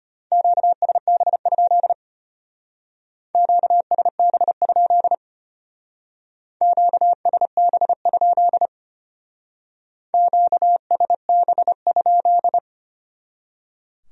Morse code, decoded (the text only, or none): QSB? QSB? QSB? QSB?